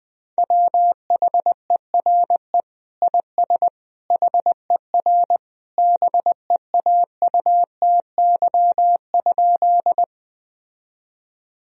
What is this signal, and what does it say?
Morse code, heard as a keyed tone, about 21 wpm, 700 Hz. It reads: WHERE IS HER BEAUTY?
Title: Sentences from Top 1000 Words - Review Part 2 - 20wpm